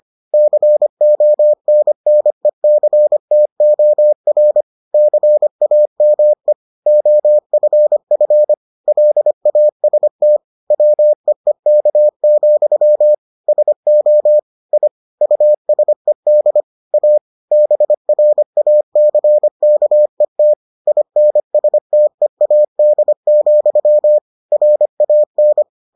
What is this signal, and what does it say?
Morse code, heard as a keyed tone, about 25 wpm, 600 Hz. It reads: CONNECTOR CAME OFF LAST WEEK, SO I USED A BRACKET INSTEAD, RAN